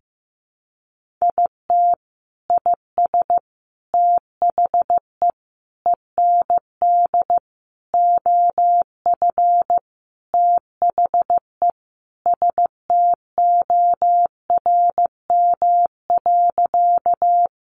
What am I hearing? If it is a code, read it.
Morse code, heard as a keyed tone, about 15 wpm, 700 Hz. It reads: IT IS THE END OF THE STORM.
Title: Sentences from Top 1500 Words 15wpm